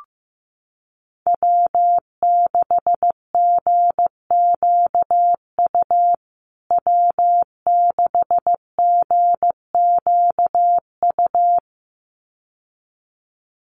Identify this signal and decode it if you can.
Morse code, heard as a keyed tone, about 15 wpm, 700 Hz. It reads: W6GQU W6GQU
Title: USA Call Signs Repeated 2x 15wpm